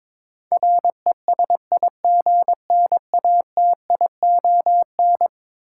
Morse code, heard as a keyed tone, about 21 wpm, 700 Hz. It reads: RESIGNATION
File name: practice / 22wpm / Eleven Letter Words - 22wpm